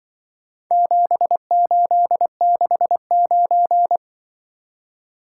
Morse code, heard as a keyed tone, about 23 wpm, 700 Hz. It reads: 7869
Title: Speed-Racing - Four Letter-Number Combinations - 20wpm